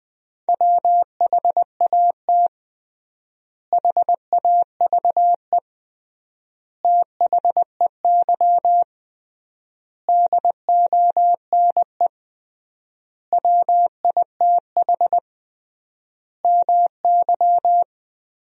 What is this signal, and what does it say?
Morse code, heard as a keyed tone, about 21 wpm, 700 Hz. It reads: WHAT HAVE THEY DONE WITH MY
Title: 3x Word-Spacing - Sentences from Top 500 Words - 20wpm